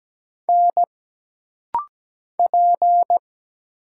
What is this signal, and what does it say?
Morse code, heard as a keyed tone, about 17 wpm, 700 Hz. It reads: N P
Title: Single Letters - Rapid-Fire 17wpm